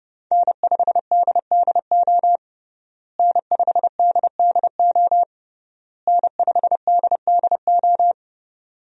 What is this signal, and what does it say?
Morse code, heard as a keyed tone, about 32 wpm, 700 Hz. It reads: N5DDO N5DDO N5DDO
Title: USA Call Signs - 5 Characters - Repeated 3x - 30wpm